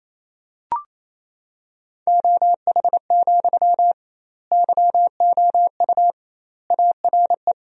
Morse code, heard as a keyed tone, about 28 wpm, 700 Hz. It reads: OH, YOU ARE
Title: 2x Word-Spacing - Sentences from Top 500 Words - 28wpm